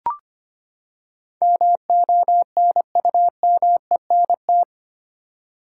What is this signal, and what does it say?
Morse code, heard as a keyed tone, about 25 wpm, 700 Hz. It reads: MONUMENT
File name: Eight Letter Words - 25wpm